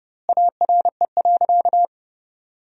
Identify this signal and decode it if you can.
Morse code, heard as a keyed tone, about 30 wpm, 700 Hz. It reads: ARE.
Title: 4x Word-Spacing - Sentences from Top 200 Words - 30wpm